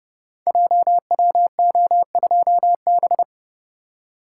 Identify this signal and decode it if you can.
Morse code, heard as a keyed tone, about 30 wpm, 700 Hz. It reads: JWO2B
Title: Code Groups - 5 Letter:Numbers - 30wpm